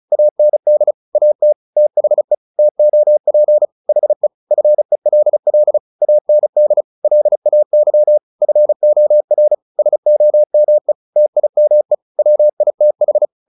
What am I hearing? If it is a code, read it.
Morse code, heard as a keyed tone, about 34 wpm, 600 Hz. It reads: AND AT THE TOP HE FELL AND LAY FOR SOME TIME WITH
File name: the old man and the sea 2 35wpm